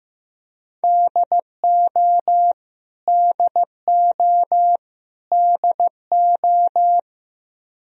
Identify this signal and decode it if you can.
Morse code, heard as a keyed tone, about 15 wpm, 700 Hz. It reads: DO DO DO
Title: Top 100 Words Repeated 15wpm